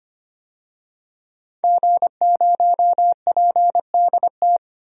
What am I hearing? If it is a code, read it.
Morse code, heard as a keyed tone, about 25 wpm, 700 Hz. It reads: G0PDT
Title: DXCC-100 Call Signs - 5 Characters - 25wpm